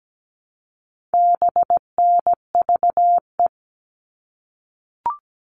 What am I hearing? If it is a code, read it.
Morse code, heard as a keyed tone, about 17 wpm, 700 Hz. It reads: BNVE
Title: All Cut Numbers - 4-Digit 17wpm